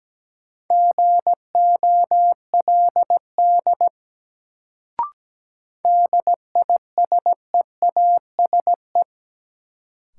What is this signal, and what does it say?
Morse code, heard as a keyed tone, about 17 wpm, 700 Hz. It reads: GOLD DISEASE
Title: Sentences from Top 1100 Words - New Word Drill Alt - 17wpm